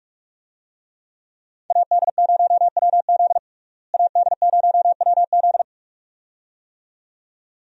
Morse code, heard as a keyed tone, about 43 wpm, 700 Hz. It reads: AD0WZ AD0WZ
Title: USA Call Signs - 5 Characters - Repeated 2x - 45wpm